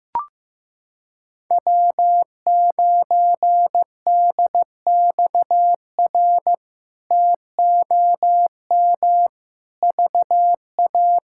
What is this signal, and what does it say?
Morse code, heard as a keyed tone, about 15 wpm, 700 Hz. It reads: W9DXR TOM VA
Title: K1USN SST QSO Practice - 15wpm